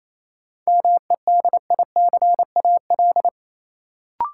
Morse code, heard as a keyed tone, about 28 wpm, 700 Hz. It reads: MEDICAL
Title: Sentences from Top 800 Words - New Word Drill - 28wpm